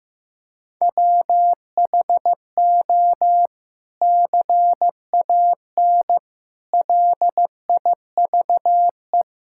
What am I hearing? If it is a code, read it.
Morse code, heard as a keyed tone, about 15 wpm, 700 Hz. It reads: WHO CAN LIVE
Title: Sentences from Top 1600 Words 15wpm